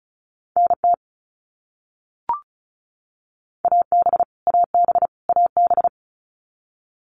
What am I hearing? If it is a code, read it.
Morse code, heard as a keyed tone, about 36 wpm, 700 Hz. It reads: NT AB AB AB
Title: Canadian Provinces and Territories - Repeated 3x - 35wpm